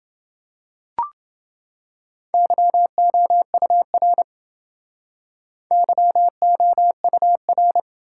Morse code, heard as a keyed tone, about 29 wpm, 700 Hz. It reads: YOUR YOUR